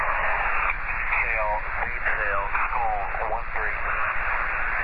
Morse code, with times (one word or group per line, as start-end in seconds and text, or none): none